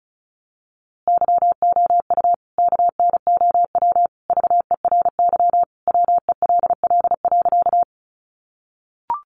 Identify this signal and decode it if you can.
Morse code, heard as a keyed tone, about 34 wpm, 700 Hz. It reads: YOU KNOW VERY WELL.
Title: A Sound Of Thunder 35wpm